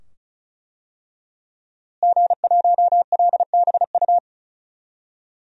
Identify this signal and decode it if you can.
Morse code, heard as a keyed tone, about 36 wpm, 700 Hz. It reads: G1LBU
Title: DXCC-100 Call Signs - 5 Characters - Repeated 2x - 35wpm